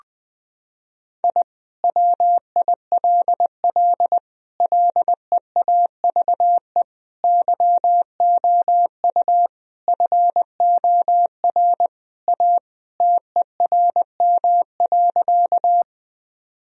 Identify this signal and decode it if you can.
Morse code, heard as a keyed tone, about 20 wpm, 700 Hz. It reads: I WILL LEAVE YOU FOR A TERM.
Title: Sentences from Top 1000 Words - Review Part 4 - 20wpm